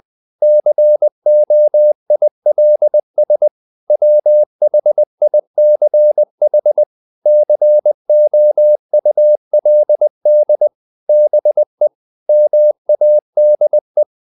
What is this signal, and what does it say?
Morse code, heard as a keyed tone, about 20 wpm, 600 Hz. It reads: COILS WHICH COULD BE MADE
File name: the old man and the sea / the old man and the sea part 1 20wpm